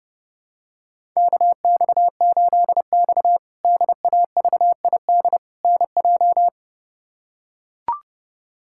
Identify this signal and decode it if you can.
Morse code, heard as a keyed tone, about 30 wpm, 700 Hz. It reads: KX8X DAVID NJ
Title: K1USN SST QSO Practice - 30wpm